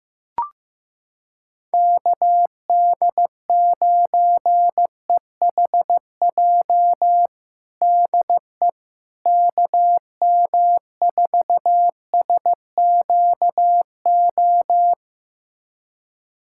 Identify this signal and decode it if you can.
Morse code, heard as a keyed tone, about 15 wpm, 700 Hz. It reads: KD9EHJ DE KM4SQO
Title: USA Callsign DE Callsign 15wpm